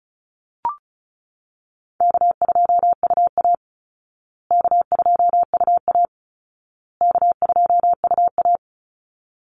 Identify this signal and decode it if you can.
Morse code, heard as a keyed tone, about 34 wpm, 700 Hz. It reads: K2UA K2UA K2UA
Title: Easy Call Signs Repeated 3x - 35wpm